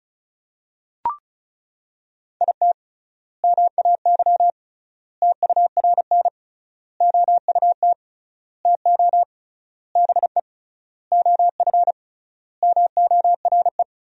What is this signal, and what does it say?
Morse code, heard as a keyed tone, about 34 wpm, 700 Hz. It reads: IT MAY TURN OUT TO BE OF MORE
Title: 3x Word-Spacing - Sentences from Top 400 Words - 35wpm